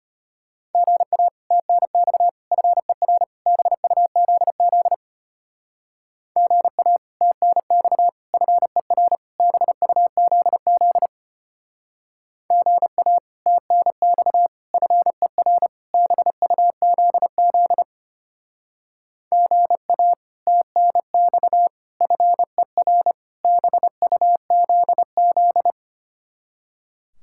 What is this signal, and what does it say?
Morse code, heard as a keyed tone, about 30 wpm, 700 Hz. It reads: GA TNX FER BUZZ GA TNX FER BUZZ GA TNX FER BUZZ GA TNX FER BUZZ
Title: Speed-Racing Common QSO Phrases Encore - 25wpm